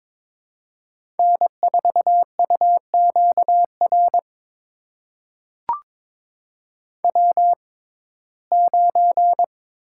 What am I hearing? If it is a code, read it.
Morse code, heard as a keyed tone, about 21 wpm, spaced as for 16 wpm, 700 Hz. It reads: N4UQR W9
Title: USA Call Signs - 5 Characters - 6x Character Spacing - 22wpm